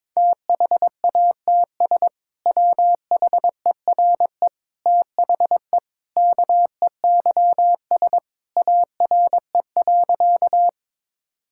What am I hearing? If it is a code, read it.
Morse code, heard as a keyed tone, about 22 wpm, 700 Hz. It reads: THATS WHERE THE KEYS ARE.
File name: practice / 22wpm / Sentences with 2-Word Phrases - 22wpm